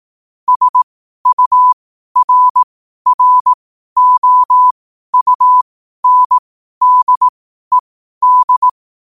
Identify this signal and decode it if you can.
Morse code, heard as a keyed tone, about 18 wpm, spaced as for 12 wpm, 1000 Hz. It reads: SURROUNDED
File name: qotdfiles / 12wpm